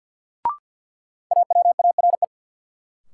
Morse code, heard as a keyed tone, about 47 wpm, 700 Hz. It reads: AWARE
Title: Sentences from Top 1200 Words - New Word Drill - 50wpm